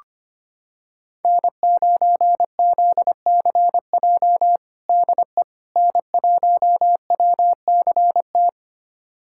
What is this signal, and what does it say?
Morse code, heard as a keyed tone, about 25 wpm, 700 Hz. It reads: N9ZCJ DE N1WCT
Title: USA Callsign DE Callsign - 5 Characters - 25wpm